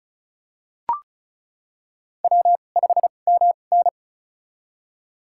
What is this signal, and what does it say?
Morse code, heard as a keyed tone, about 34 wpm, spaced as for 25 wpm, 700 Hz. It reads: W5MN
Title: USA Call Signs - 2x Character Spacing - 35wpm